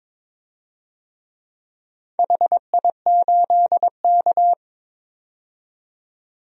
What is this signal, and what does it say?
Morse code, heard as a keyed tone, about 23 wpm, 700 Hz. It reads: HI8K
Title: DXCC-100 Call Signs - 4 Characters - 22wpm